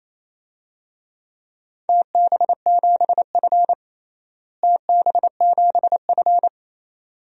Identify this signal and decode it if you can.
Morse code, heard as a keyed tone, about 28 wpm, 700 Hz. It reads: TB7F TB7F
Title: DXCC-100 Call Signs - 4 Characters - Repeated 2x - 28wpm